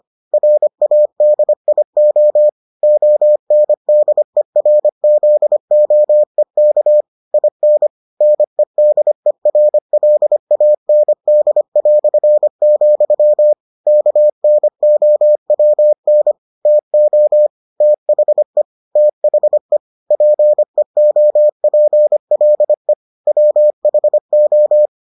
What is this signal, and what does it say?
Morse code, heard as a keyed tone, about 25 wpm, 600 Hz. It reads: RADIO ONDERZOEK IN NEDERLAND", KNOWN TO THE THE PEOPLE WHO